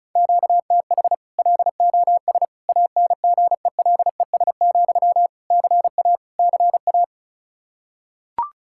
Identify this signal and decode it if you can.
Morse code, heard as a keyed tone, about 36 wpm, 700 Hz. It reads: QTH LOS ANGELES, CA CA
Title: Practice QSO - Exchange 1 - Beginning - 35wpm